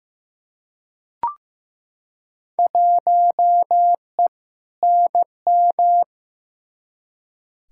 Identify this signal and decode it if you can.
Morse code, heard as a keyed tone, about 15 wpm, 700 Hz. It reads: 1E NM
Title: Field Day Exchanges - 15wpm